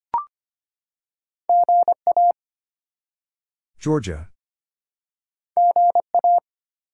Morse code, heard as a keyed tone, about 25 wpm, 700 Hz. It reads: GA GA
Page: morse-code-ninja-practice-25wpm